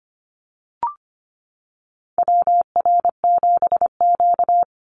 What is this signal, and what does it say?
Morse code, heard as a keyed tone, about 25 wpm, 700 Hz. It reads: WR7Q
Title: Call Signs - 25wpm